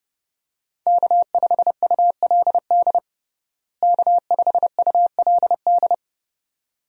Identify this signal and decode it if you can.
Morse code, heard as a keyed tone, about 32 wpm, 700 Hz. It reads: K5ULD K5ULD
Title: USA Call Signs - 5 Characters - Repeated 2x - 30wpm